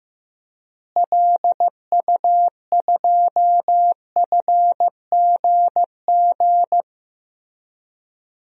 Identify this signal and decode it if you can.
Morse code, heard as a keyed tone, about 15 wpm, 700 Hz. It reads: LU2FGG
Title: DXCC-100 Call Signs - 6 Characters - Repeated 2x - 15wpm